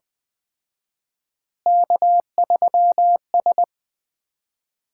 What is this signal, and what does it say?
Morse code, heard as a keyed tone, about 20 wpm, 700 Hz. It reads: K3S
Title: Common QSO Phrases - 4x Word-Spacing - 20wpm